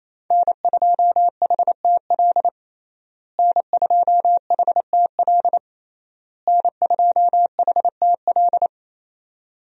Morse code, heard as a keyed tone, about 28 wpm, 700 Hz. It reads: N2HTL N2HTL N2HTL
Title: USA Call Signs - 5 Characters - Repeated 3x - 28wpm